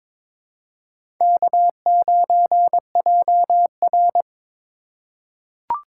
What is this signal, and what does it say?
Morse code, heard as a keyed tone, about 21 wpm, 700 Hz. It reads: K9JR
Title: USA Call Signs - 4 Characters - 4x Character Spacing - 22wpm